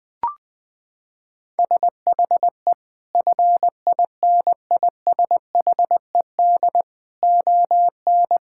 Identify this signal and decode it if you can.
Morse code, heard as a keyed tone, about 20 wpm, 700 Hz. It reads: SHE FINISHED ON
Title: Sentences with 8-Word Phrases - 20wpm